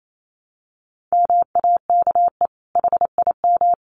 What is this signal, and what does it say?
Morse code, heard as a keyed tone, about 28 wpm, 700 Hz. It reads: MAKE HIM
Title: Sentences from Top 900 Words 28wpm